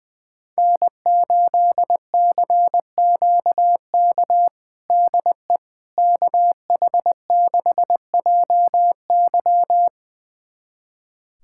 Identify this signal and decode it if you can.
Morse code, heard as a keyed tone, about 20 wpm, 700 Hz. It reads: N8CQK DE KH6JY
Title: USA Callsign DE Callsign - 5 Characters - 20wpm